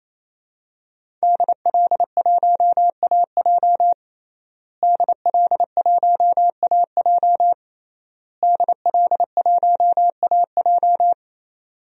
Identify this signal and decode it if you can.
Morse code, heard as a keyed tone, about 27 wpm, 700 Hz. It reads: DL1AJ DL1AJ DL1AJ
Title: DXCC-100 Call Signs - 5 Characters - Repeated 3x - 28wpm